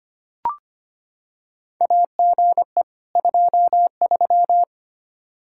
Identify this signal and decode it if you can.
Morse code, heard as a keyed tone, about 25 wpm, 700 Hz. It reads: AGE 23
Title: Practice QSO - Exchange 3 - More Info - 25wpm